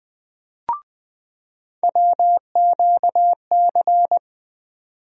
Morse code, heard as a keyed tone, about 20 wpm, 700 Hz. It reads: WQC
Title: USA Call Sign Postfixes - 3 Letters 20wpm